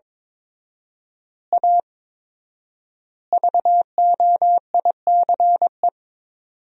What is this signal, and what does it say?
Morse code, heard as a keyed tone, about 22 wpm, 700 Hz. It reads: A VOICE
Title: 4x Word-Spacing - Sentences from Top 500 Words - 22wpm